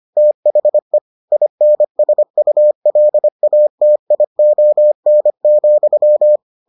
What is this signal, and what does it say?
Morse code, heard as a keyed tone, about 25 wpm, 600 Hz. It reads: THE INSULATION,